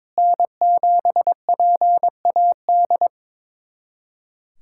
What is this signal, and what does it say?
Morse code, heard as a keyed tone, about 22 wpm, 700 Hz. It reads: N7PAD
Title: USA Callsign DE Callsign - 5 Characters - 22wpm